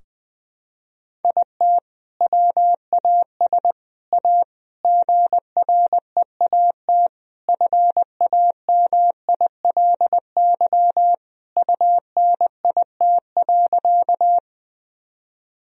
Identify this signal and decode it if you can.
Morse code, heard as a keyed tone, about 20 wpm, 700 Hz. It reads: IT WAS A GREAT FAMILY UNIT.
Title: Sentences from Top 1000 Words - Review Part 2 - 20wpm